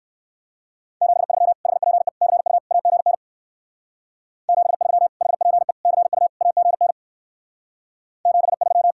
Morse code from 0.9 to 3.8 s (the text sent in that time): 73 HPE CU AGN